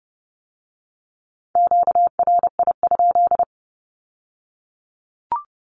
Morse code, heard as a keyed tone, about 30 wpm, 700 Hz. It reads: QRI?
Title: Q-Codes with Spoken Meaning - 30wpm